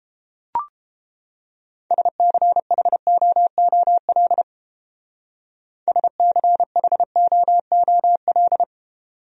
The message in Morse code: SCHOOL SCHOOL

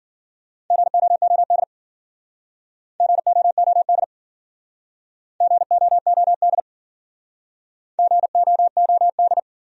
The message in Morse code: GOOD GOOD GOOD GOOD

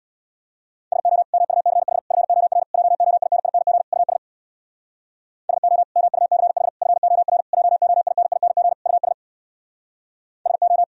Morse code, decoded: SO MUCH FOR JOINING US SO MUCH FOR JOINING US SO

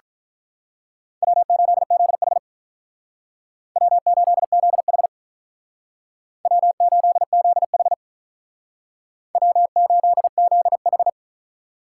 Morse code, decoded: W8ZH W8ZH W8ZH W8ZH

